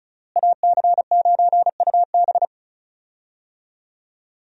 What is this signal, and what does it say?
Morse code, heard as a keyed tone, about 36 wpm, 700 Hz. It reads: AC9UB
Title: USA Call Signs - 5 Characters - Repeated 2x - 35wpm